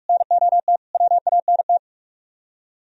Morse code, heard as a keyed tone, about 58 wpm, 700 Hz. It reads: NOT WANT